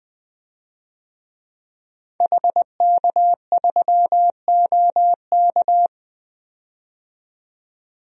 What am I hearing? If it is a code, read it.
Morse code, heard as a keyed tone, about 20 wpm, 700 Hz. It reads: HK3OK
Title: DXCC-100 Call Signs - 5 Characters - 20wpm